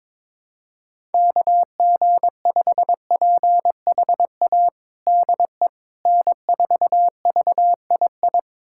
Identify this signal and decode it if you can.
Morse code, heard as a keyed tone, about 22 wpm, 700 Hz. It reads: KG5PHA DE N4VII
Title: USA Callsign DE Callsign 22wpm